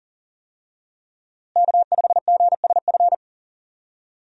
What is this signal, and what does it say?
Morse code, heard as a keyed tone, about 40 wpm, 700 Hz. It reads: K5GSF
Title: USA Call Signs - 5 Characters - Repeated 3x - 40wpm